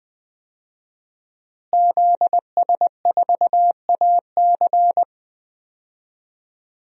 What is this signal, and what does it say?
Morse code, heard as a keyed tone, about 20 wpm, 700 Hz. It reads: ZS4AC
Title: DXCC-100 Call Signs - 5 Characters - Repeated 2x - 20wpm